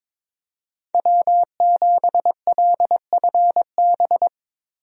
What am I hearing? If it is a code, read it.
Morse code, heard as a keyed tone, about 22 wpm, 700 Hz. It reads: W7LFB
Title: USA Call Signs - 5 Characters - Repeated 3x - 22wpm